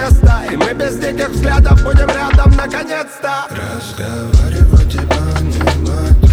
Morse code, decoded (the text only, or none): none